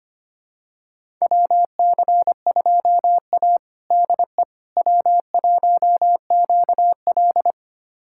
Morse code, heard as a keyed tone, about 25 wpm, 700 Hz. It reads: WC2A DE W1QL
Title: USA Callsign DE Callsign - 4 Characters - 25wpm